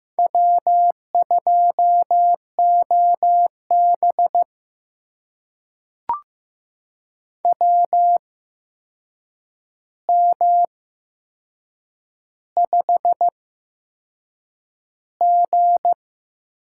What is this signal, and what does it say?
Morse code, heard as a keyed tone, about 15 wpm, spaced as for 6 wpm, 700 Hz. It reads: W2OB WM5G
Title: USA Call Signs - 4 Characters - 8x Character Spacing - 15wpm